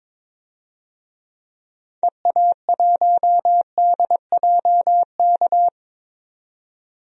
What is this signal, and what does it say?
Morse code, heard as a keyed tone, about 22 wpm, 700 Hz. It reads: EA1DJK
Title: DXCC-100 Call Signs - 6 Characters - 22wpm